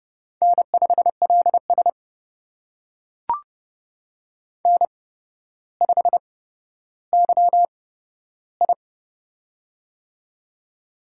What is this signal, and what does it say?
Morse code, heard as a keyed tone, about 29 wpm, spaced as for 13 wpm, 700 Hz. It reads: N5LS N5YI
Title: USA Call Signs - 4 Characters - 8x Character Spacing - 30wpm